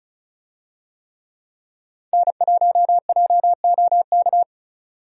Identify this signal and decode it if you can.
Morse code, heard as a keyed tone, about 36 wpm, 700 Hz. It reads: N1JOK
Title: USA Call Signs - 5 Characters - 35wpm